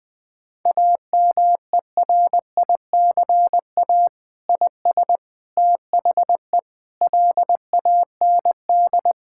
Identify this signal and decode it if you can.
Morse code, heard as a keyed tone, about 19 wpm, 700 Hz. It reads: AMERICA IS THE LAND
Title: Sentences with 8-Word Phrases - 20wpm